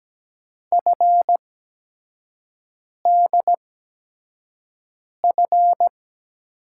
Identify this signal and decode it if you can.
Morse code, heard as a keyed tone, about 17 wpm, spaced as for 5 wpm, 700 Hz. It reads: FDF
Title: USA Call Signs - 6 Characters - 8x Character Spacing - 17wpm